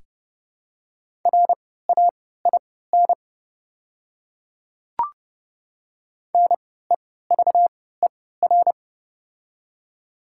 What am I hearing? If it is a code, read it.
Morse code, heard as a keyed tone, about 30 wpm, spaced as for 17 wpm, 700 Hz. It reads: RAIN NEVER
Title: 3x Character Spacing - Top 300 Words - 30wpm